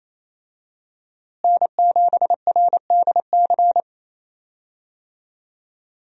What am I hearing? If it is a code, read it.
Morse code, heard as a keyed tone, about 28 wpm, 700 Hz. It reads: N7RDC
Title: USA Call Signs - 5 Characters - Repeated 2x - 28wpm